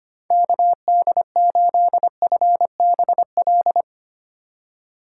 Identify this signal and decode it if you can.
Morse code, heard as a keyed tone, about 25 wpm, 700 Hz. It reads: KD8FBL
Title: USA Call Signs - 3x Character Spacing - 25wpm